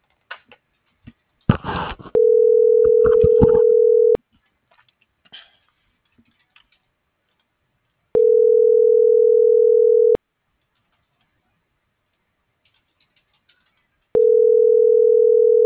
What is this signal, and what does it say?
Ringback tone (North America), 440 + 480 Hz.